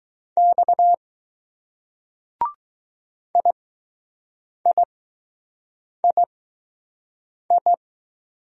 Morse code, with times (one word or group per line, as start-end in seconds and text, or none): 0.4-0.9 X
3.4-7.7 IIII